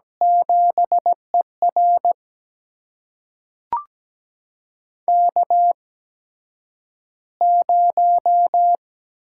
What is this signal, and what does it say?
Morse code, heard as a keyed tone, about 17 wpm, spaced as for 9 wpm, 700 Hz. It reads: 7ER K0
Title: USA Call Signs - 4 Characters - 8x Character Spacing - 17wpm